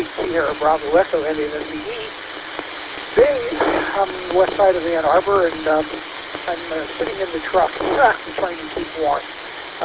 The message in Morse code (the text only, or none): none